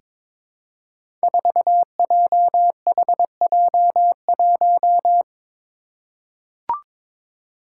Morse code, heard as a keyed tone, about 21 wpm, 700 Hz. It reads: 4JHJ1
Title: Five Challenging Letter-Number Combinations 22wpm